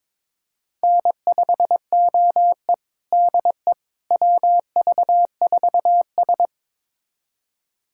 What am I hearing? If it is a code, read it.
Morse code, heard as a keyed tone, about 22 wpm, 700 Hz. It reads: N5OE DE WV4S
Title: USA Callsign DE Callsign - 4 Characters - 22wpm